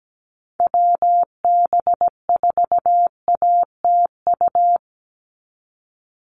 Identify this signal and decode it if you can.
Morse code, heard as a keyed tone, about 17 wpm, 700 Hz. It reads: WB4ATU
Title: USA Call Signs 17wpm